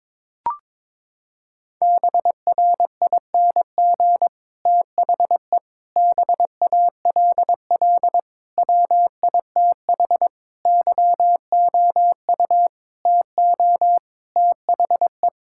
Morse code, heard as a keyed tone, about 22 wpm, 700 Hz. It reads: BRING THE BALL WITH YOU TO THE
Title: Sentences from Top 1000 Words - Review Part 6 - 22wpm